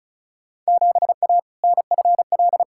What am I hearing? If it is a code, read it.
Morse code, heard as a keyed tone, about 36 wpm, 700 Hz. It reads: 7A NFL
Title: Field Day Exchanges Encore - 35wpm